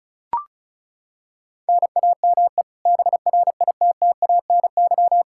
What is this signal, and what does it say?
Morse code, heard as a keyed tone, about 36 wpm, 700 Hz. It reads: NAME BRITTANY
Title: Practice QSO - Exchange 1 - Beginning - 35wpm